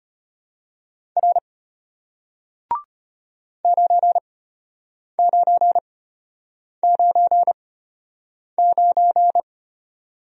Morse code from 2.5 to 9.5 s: 9999

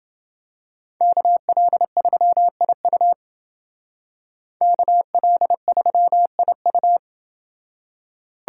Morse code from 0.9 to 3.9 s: KL3IU

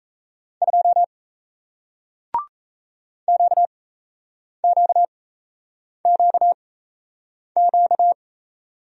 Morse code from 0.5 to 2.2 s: 2

